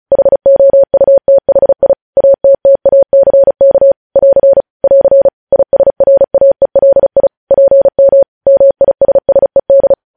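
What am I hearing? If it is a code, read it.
Morse code, heard as a keyed tone, about 36 wpm, 550 Hz. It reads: HOUTHI ATTACK <AR> <AR> ISRAELI PM MISSED
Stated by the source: Morse news headlines sent at 35 words per minute.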